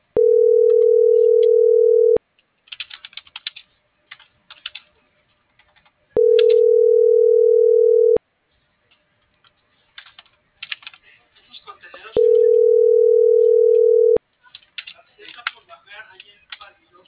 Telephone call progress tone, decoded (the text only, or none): ringback tone